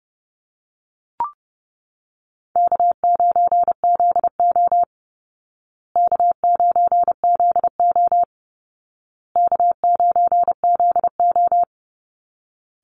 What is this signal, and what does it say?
Morse code, heard as a keyed tone, about 28 wpm, 700 Hz. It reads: K9ZO K9ZO K9ZO
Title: Easy Call Signs Repeated 3x - 30wpm